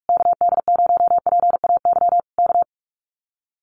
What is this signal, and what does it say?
Morse code, heard as a keyed tone, about 44 wpm, 700 Hz. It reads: KD0PAY K
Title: Common QSO Phrases Encore 45wpm